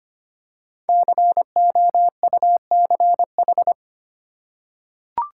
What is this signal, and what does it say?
Morse code, heard as a keyed tone, about 25 wpm, 700 Hz. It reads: COUCH
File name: practice / 25wpm / Easy Words with Hints - 25wpm